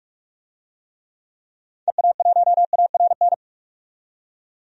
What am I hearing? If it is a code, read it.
Morse code, heard as a keyed tone, about 46 wpm, 700 Hz. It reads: EA1ARN
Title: DXCC-100 Call Signs - 6 Characters - 45wpm